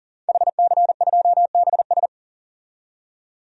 Morse code, heard as a keyed tone, about 40 wpm, 700 Hz. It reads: HC2BS
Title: DXCC-100 Call Signs - 5 Characters - 40wpm